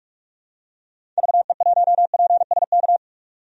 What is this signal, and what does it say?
Morse code, heard as a keyed tone, about 46 wpm, 700 Hz. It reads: VE1PSK
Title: DXCC-100 Call Signs - 6 Characters - 45wpm